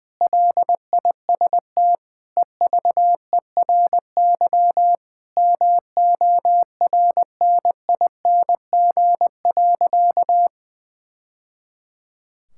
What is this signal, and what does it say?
Morse code, heard as a keyed tone, about 20 wpm, 700 Hz. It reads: LIST EVERY MORNING.
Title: Sentences from Top 1000 Words - Review Part 6 - 20wpm